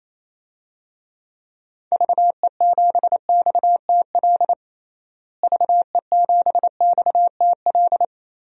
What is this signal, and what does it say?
Morse code, heard as a keyed tone, about 28 wpm, 700 Hz. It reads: VE7XTL VE7XTL
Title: DXCC-100 Call Signs - 6 Characters - Repeated 3x - 28wpm